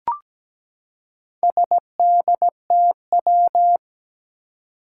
Morse code, heard as a keyed tone, about 17 wpm, 700 Hz. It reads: SDTW
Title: Four Challenging Letter-Number Combinations 17wpm